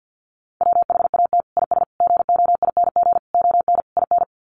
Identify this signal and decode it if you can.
Morse code, heard as a keyed tone, about 50 wpm, 700 Hz. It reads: WHAT IS GOING ON IN
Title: Sets of 5 Words Encore2 50wpm